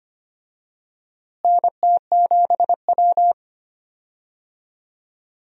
Morse code, heard as a keyed tone, about 25 wpm, 700 Hz. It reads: NT7W
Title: USA Call Signs - 4 Characters - Repeated 2x - 25wpm